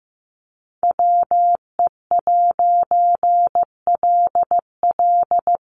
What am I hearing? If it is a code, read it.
Morse code, heard as a keyed tone, about 15 wpm, 700 Hz. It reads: WE'LL
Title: A Sound Of Thunder 15wpm